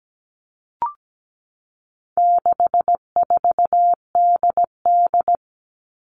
Morse code, heard as a keyed tone, about 17 wpm, 700 Hz. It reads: 64DD
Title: All Cut Numbers - 4-Digit 17wpm